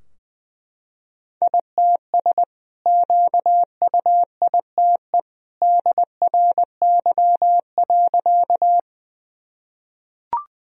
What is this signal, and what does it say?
Morse code, heard as a keyed tone, about 21 wpm, 700 Hz. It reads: ITS QUITE DRY.
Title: Sentences from Top 1000 Words - Review Part 3 - 20wpm